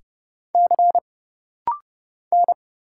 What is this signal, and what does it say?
Morse code, heard as a keyed tone, about 30 wpm, 700 Hz. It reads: C N
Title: Single Letters - Rapid-Fire 30wpm